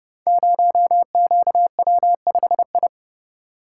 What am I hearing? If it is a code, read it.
Morse code, heard as a keyed tone, about 28 wpm, 700 Hz. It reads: 0QW5I
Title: Code Groups - 5 Letter:Numbers - 30wpm